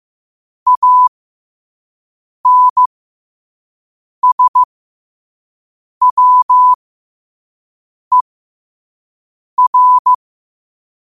Morse code, heard as a keyed tone, about 15 wpm, spaced as for 5 wpm, 1000 Hz. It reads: ANSWER
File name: qotdfiles / 5wpm